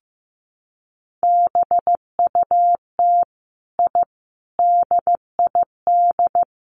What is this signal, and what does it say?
Morse code, heard as a keyed tone, about 15 wpm, 700 Hz. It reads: BUT I DID
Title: Sets of 5 Words Encore 15wpm